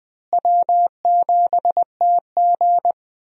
Morse code, heard as a keyed tone, about 20 wpm, 700 Hz. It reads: W7TG
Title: USA Call Signs - 4 Characters - Repeated 2x - 20wpm